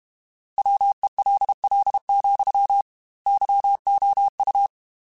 Morse code, heard as a keyed tone, about 32 wpm, 800 Hz. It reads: WELL, YOU
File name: Technical Subjects / Technical Subjects-mp3s / AND Gate @30